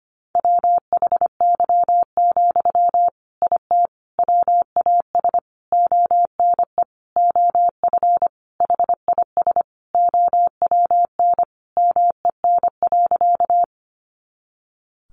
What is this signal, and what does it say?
Morse code, heard as a keyed tone, about 25 wpm, 700 Hz. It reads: WHY, IT WAS ONE OF HIS OWN MEN.
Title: Sentences from Top 300 Words 25wpm